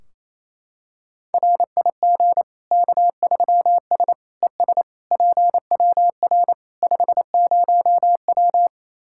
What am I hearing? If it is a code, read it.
Morse code, heard as a keyed tone, about 27 wpm, 700 Hz. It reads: RIG K3S ES PWR 50W
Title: Practice QSO - Exchange 2 - Info - 28wpm